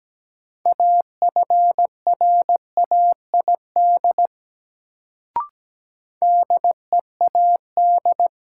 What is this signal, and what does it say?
Morse code, heard as a keyed tone, about 17 wpm, 700 Hz. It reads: AFRAID DEAD
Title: Sentences from Top 1500 Words - New Word Drill - 17wpm